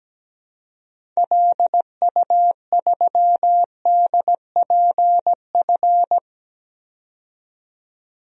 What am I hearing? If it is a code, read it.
Morse code, heard as a keyed tone, about 17 wpm, 700 Hz. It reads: LU3DPF
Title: DXCC-100 Call Signs - 6 Characters - 17wpm